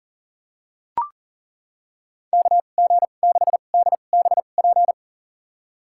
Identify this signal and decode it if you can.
Morse code, heard as a keyed tone, about 37 wpm, spaced as for 28 wpm, 700 Hz. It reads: KG6DBP
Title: USA Call Signs - 2x Character Spacing - 40wpm